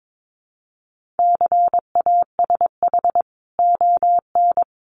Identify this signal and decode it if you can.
Morse code, heard as a keyed tone, about 22 wpm, 700 Hz. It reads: CASH ON
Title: Sentences from Top 1500 Words 22wpm